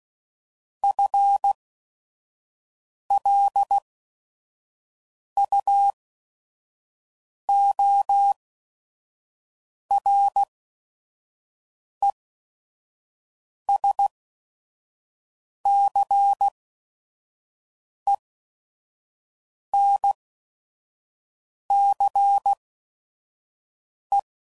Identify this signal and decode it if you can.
Morse code, heard as a keyed tone, about 16 wpm, spaced as for 5 wpm, 800 Hz. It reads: FLUORESCENCE